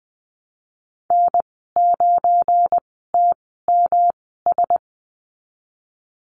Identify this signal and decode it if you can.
Morse code, heard as a keyed tone, about 20 wpm, spaced as for 14 wpm, 700 Hz. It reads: N9TMS
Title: 238 Lesson 38g - Letter q - Call Signs 20wpm